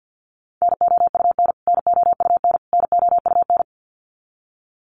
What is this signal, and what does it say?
Morse code, heard as a keyed tone, about 50 wpm, 700 Hz. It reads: NOUN NOUN NOUN
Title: Top 500 Words Repeated 50wpm